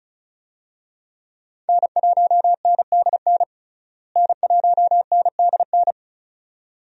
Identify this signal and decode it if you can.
Morse code, heard as a keyed tone, about 36 wpm, 700 Hz. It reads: N1NDN N1NDN
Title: USA Call Signs - 5 Characters - Repeated 2x - 35wpm